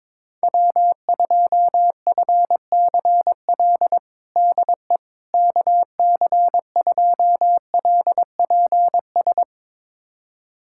Morse code, heard as a keyed tone, about 22 wpm, 700 Hz. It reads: W2FCL DE KC2LPS
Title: USA Callsign DE Callsign 22wpm